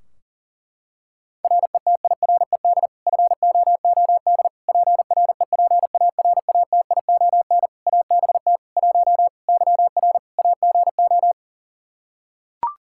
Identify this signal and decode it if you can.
Morse code, heard as a keyed tone, about 40 wpm, 700 Hz. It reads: RETIRED FOOD PREPARATION ABT 1 YR AGO
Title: Practice QSO - Exchange 3 - More Info - 40wpm